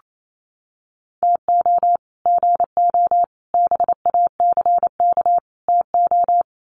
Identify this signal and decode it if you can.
Morse code, heard as a keyed tone, about 28 wpm, 700 Hz. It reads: TO GO BACK TO
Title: Sets of 4 Words 28wpm